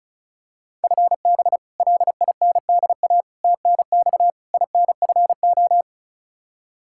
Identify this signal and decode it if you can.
Morse code, heard as a keyed tone, about 36 wpm, 700 Hz. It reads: FB LINDA TNX INFO
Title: Practice QSO - Exchange 3 - More Info - 35wpm